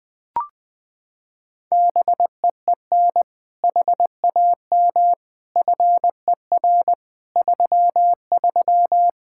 Morse code, heard as a keyed tone, about 20 wpm, 700 Hz. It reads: BEEN HAM FER 33
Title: Practice QSO - Exchange 3 - More Info - 20wpm